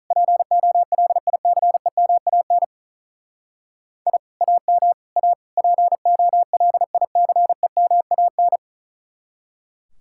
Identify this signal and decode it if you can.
Morse code, heard as a keyed tone, about 42 wpm, 700 Hz. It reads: POLICEMAN I AM A POLICEMAN
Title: Speed-Racing Common QSO Phrases Encore - 35wpm